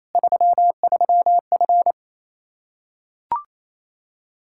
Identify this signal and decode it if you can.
Morse code, heard as a keyed tone, about 28 wpm, 700 Hz. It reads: 33F
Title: Practice QSO - Exchange 2 - Info - 28wpm